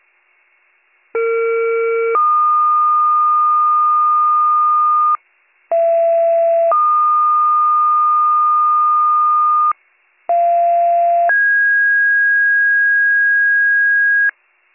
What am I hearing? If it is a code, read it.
Two-tone page, heard as a one-second tone, then a three-second tone, about 460 then 1220 Hz, about 670 then 1220 Hz, about 690 then 1650 Hz. It is heard from 1.1 to 14.3 s.